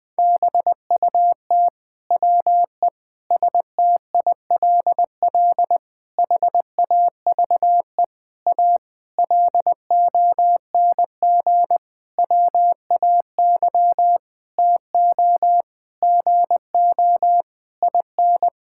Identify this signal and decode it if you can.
Morse code, heard as a keyed tone, about 20 wpm, 700 Hz. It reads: BUT WE STILL HAVE A LONG WAY TO GO IN
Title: Sentences with 9-Word Phrases - Spoken Beginning - 20wpm